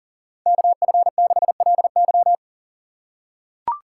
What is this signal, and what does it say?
Morse code, heard as a keyed tone, about 39 wpm, 700 Hz. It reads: KF6LY
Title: USA Call Signs - 5 Characters - 6x Character Spacing - 40wpm